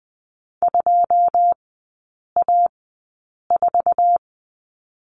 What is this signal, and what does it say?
Morse code, heard as a keyed tone, about 20 wpm, spaced as for 8 wpm, 700 Hz. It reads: 2A4
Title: 100 Lesson 17e - Letter u - Three Characters 20wpm